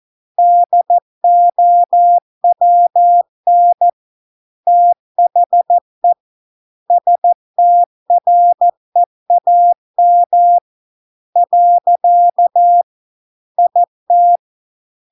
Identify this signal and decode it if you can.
Morse code, heard as a keyed tone, about 14 wpm, 700 Hz. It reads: DOWN THE STREAM . IT